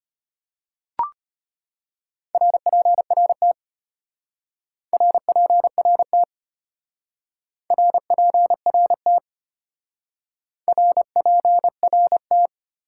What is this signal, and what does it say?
Morse code, heard as a keyed tone, about 33 wpm, 700 Hz. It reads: RPRT RPRT RPRT RPRT